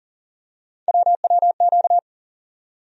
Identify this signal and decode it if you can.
Morse code, heard as a keyed tone, about 40 wpm, 700 Hz. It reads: WWQ
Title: USA Call Sign Postfixes - 3 Letters 40wpm